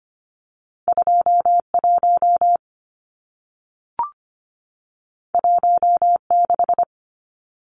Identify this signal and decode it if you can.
Morse code, heard as a keyed tone, about 25 wpm, 700 Hz. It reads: 21 16
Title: Numbers - 2-Digit 25wpm